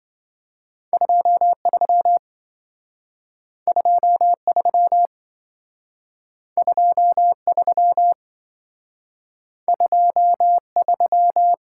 Morse code, heard as a keyed tone, about 26 wpm, 700 Hz. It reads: 23 23 23 23